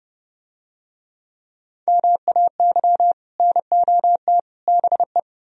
Morse code, heard as a keyed tone, about 31 wpm, 700 Hz. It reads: MAY NOT BE